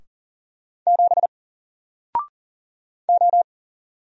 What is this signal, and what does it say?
Morse code, heard as a keyed tone, about 40 wpm, 700 Hz. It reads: 7 O
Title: Single Character (Letter:Number:Punctuation:Prosign) Alt 40wpm